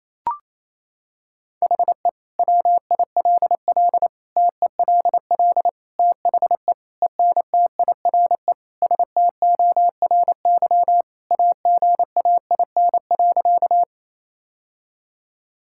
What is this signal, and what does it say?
Morse code, heard as a keyed tone, about 27 wpm, 700 Hz. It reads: HE WILL TELL THE ENTIRE STORY AGAIN.
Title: Sentences from Top 1000 Words - Review Part 5 - 28wpm